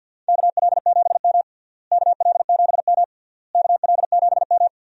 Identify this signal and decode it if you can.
Morse code, heard as a keyed tone, about 52 wpm, 700 Hz. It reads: KL7M KL7M KL7M
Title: DXCC-100 Call Signs - 4 Characters - Repeated 3x - 50wpm